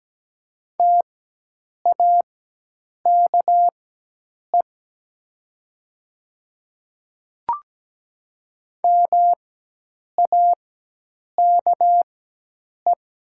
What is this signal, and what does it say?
Morse code, heard as a keyed tone, about 17 wpm, spaced as for 8 wpm, 700 Hz. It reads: TAKE MAKE
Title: 4x Character Spacing - Top 300 Words - 17wpm